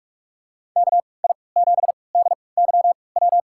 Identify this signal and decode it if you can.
Morse code, heard as a keyed tone, about 43 wpm, spaced as for 25 wpm, 700 Hz. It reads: KI7DYW
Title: USA Call Signs - 3x Character Spacing - 45wpm